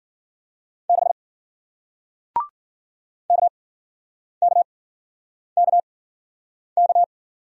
Morse code, heard as a keyed tone, about 54 wpm, spaced as for 9 wpm, 700 Hz. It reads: 6 KKKK